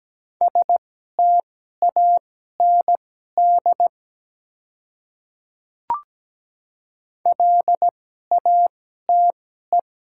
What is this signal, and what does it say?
Morse code, heard as a keyed tone, about 17 wpm, spaced as for 12 wpm, 700 Hz. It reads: STAND LATE
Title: 2x Character Spacing - Top 300 Words - 17wpm